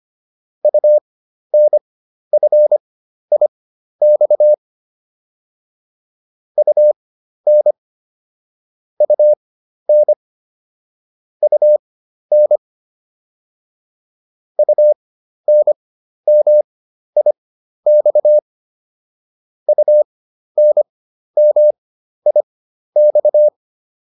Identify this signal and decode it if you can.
Morse code, heard as a keyed tone, about 25 wpm, spaced as for 12 wpm, 600 Hz. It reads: UNFIX UN UN UN UNMIX UNMIX